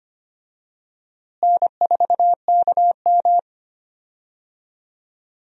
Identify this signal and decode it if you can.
Morse code, heard as a keyed tone, about 25 wpm, 700 Hz. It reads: N4KM
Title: USA Call Signs - 4 Characters - Repeated 2x - 25wpm